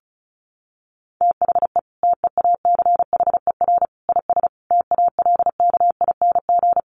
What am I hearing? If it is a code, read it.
Morse code, heard as a keyed tone, about 34 wpm, 700 Hz. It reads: THE TEACHER IS TALKING
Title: Easy Phrases 35wpm